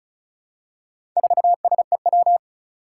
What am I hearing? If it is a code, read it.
Morse code, heard as a keyed tone, about 36 wpm, 700 Hz. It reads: 4SEW
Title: Four Challenging Letter-Number Combinations 35wpm